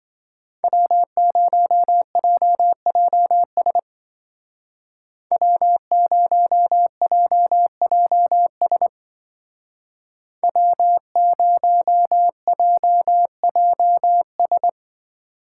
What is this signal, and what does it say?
Morse code, heard as a keyed tone, about 25 wpm, 700 Hz. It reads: W0JJS W0JJS W0JJS
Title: Speed Racing - USA 1x3 Call Signs 20wpm